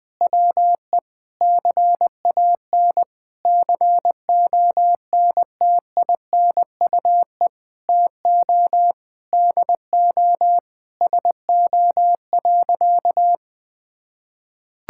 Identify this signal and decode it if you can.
Morse code, heard as a keyed tone, about 21 wpm, 700 Hz. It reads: WE CAN CONTINUE TO DO SO.
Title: Sentences from Top 1000 Words - Review Part 6 - 20wpm